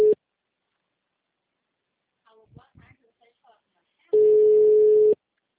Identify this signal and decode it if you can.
Ringback tone (Europe, 425 Hz), with speech.